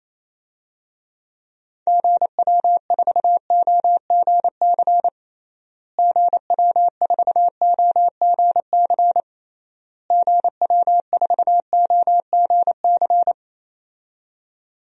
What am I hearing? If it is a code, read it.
Morse code, heard as a keyed tone, about 28 wpm, 700 Hz. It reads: GW4OGC GW4OGC GW4OGC
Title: DXCC-100 Call Signs - 6 Characters - Repeated 3x - 28wpm